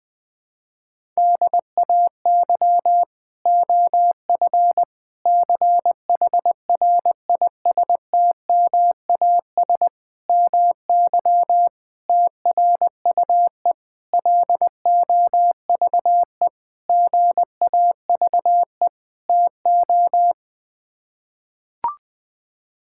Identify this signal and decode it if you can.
Morse code, heard as a keyed tone, about 20 wpm, 700 Hz. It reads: DAY OF CHRISTMAS MY TRUE LOVE GAVE TO
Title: Sets of 8 Words - 20wpm